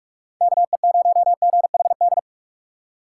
Morse code, heard as a keyed tone, about 43 wpm, 700 Hz. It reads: KE0GHD